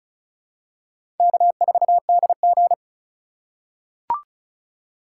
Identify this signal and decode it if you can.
Morse code, heard as a keyed tone, about 36 wpm, 700 Hz. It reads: K4DG
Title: USA Call Signs - 3x Character Spacing - 35wpm